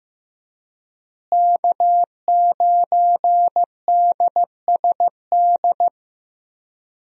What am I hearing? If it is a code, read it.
Morse code, heard as a keyed tone, about 15 wpm, 700 Hz. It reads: K9DSD
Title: USA Call Signs - 5 Characters - Repeated 3x - 15wpm